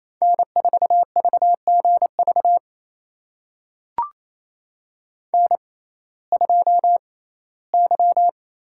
Morse code, heard as a keyed tone, about 28 wpm, spaced as for 17 wpm, 700 Hz. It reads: N4VGV N2Y
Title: USA Call Signs - 5 Characters - 6x Character Spacing - 28wpm